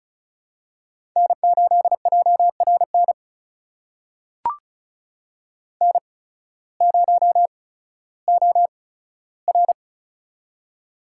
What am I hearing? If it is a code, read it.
Morse code, heard as a keyed tone, about 36 wpm, spaced as for 16 wpm, 700 Hz. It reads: N8JRN N0OR
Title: USA Call Signs - 8x Character Spacing - 35wpm